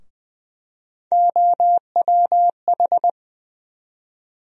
Morse code, heard as a keyed tone, about 20 wpm, 700 Hz. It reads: OWH
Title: USA Call Sign Postfixes - 3 Letters 20wpm